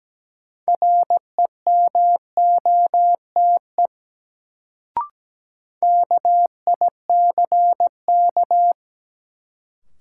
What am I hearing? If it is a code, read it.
Morse code, heard as a keyed tone, about 17 wpm, 700 Hz. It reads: REMOTE KICK
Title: Sentences from Top 1300 Words - New Word Drill - 17wpm